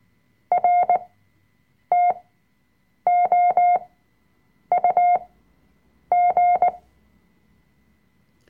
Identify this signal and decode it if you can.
Morse code, heard as a keyed tone, about 18 wpm, spaced as for 7 wpm, 710 Hz. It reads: RTOUG
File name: Workshop Dateien / CW-Online-Kurs / Lektion 005